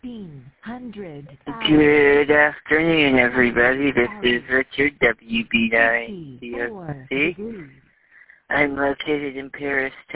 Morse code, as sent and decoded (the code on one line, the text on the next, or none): none